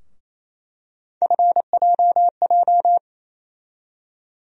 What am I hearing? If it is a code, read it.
Morse code, heard as a keyed tone, about 28 wpm, 700 Hz. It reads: FJJ
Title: USA Call Sign Postfixes - 3 Letters 28wpm